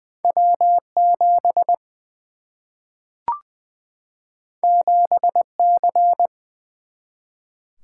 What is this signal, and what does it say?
Morse code, heard as a keyed tone, about 20 wpm, 700 Hz. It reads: W 7 7 C
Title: Single-Letter Single-Number 20wpm